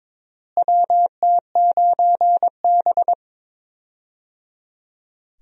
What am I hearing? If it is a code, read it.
Morse code, heard as a keyed tone, about 22 wpm, 700 Hz. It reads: WT9B
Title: USA Call Signs - 4 Characters - Repeated 3x - 22wpm